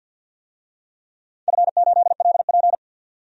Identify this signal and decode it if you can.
Morse code, heard as a keyed tone, about 52 wpm, 700 Hz. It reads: F8LP
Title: DXCC-100 Call Signs - 4 Characters - 50wpm